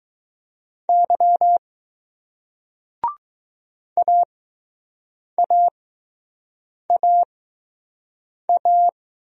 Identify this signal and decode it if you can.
Morse code, heard as a keyed tone, about 20 wpm, spaced as for 6 wpm, 700 Hz. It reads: Y AAAA